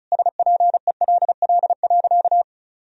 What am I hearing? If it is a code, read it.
Morse code, heard as a keyed tone, about 36 wpm, 700 Hz. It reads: SPELL.
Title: Sentences from Top 1000 Words - Review Part 1 - 35wpm